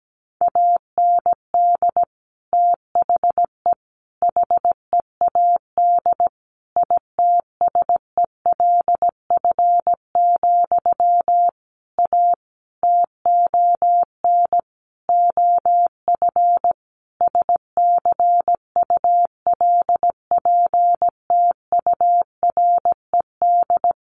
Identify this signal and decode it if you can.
Morse code, heard as a keyed tone, about 17 wpm, 700 Hz. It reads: AND THE HEAD ITSELF, A TON OF SCULPTURED
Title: A Sound Of Thunder 17wpm